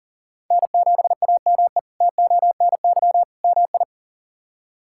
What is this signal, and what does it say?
Morse code, heard as a keyed tone, about 37 wpm, 690 Hz. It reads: N7AME TONY MI
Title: K1USN SST QSO Practice - 40wpm